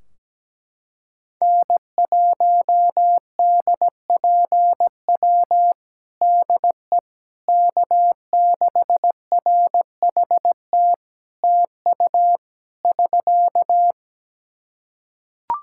Morse code, heard as a keyed tone, about 17 wpm, 700 Hz. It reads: N1DPW DE K6RHT TU <SK>
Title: Practice QSO - Exchange 4 - Ending - 17wpm